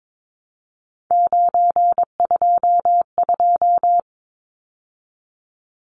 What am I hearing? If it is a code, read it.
Morse code, heard as a keyed tone, about 22 wpm, 700 Hz. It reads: 922
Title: Numbers - 3-Digit 22wpm